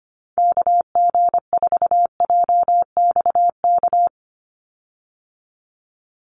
Morse code, heard as a keyed tone, about 25 wpm, 700 Hz. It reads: KG4JXK
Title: USA Call Signs 25wpm